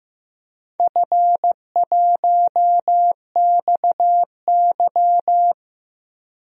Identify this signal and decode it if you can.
Morse code, heard as a keyed tone, about 15 wpm, 700 Hz. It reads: F1XY
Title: DXCC-100 Call Signs - 4 Characters - 15wpm